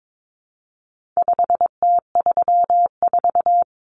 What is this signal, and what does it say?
Morse code, heard as a keyed tone, about 22 wpm, 700 Hz. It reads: 5T34
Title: Common Cut Numbers - 4-Digit 22wpm